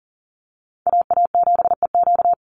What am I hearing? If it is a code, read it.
Morse code, heard as a keyed tone, about 40 wpm, 700 Hz. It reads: AA7EQ
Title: NA SOTA Chaser Callsigns 40wpm